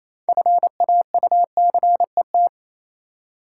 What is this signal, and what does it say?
Morse code, heard as a keyed tone, about 28 wpm, 700 Hz. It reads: FAUCET